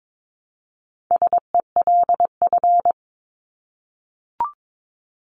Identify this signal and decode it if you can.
Morse code, heard as a keyed tone, about 22 wpm, 700 Hz. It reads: SELF
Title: Top 500 Words 22wpm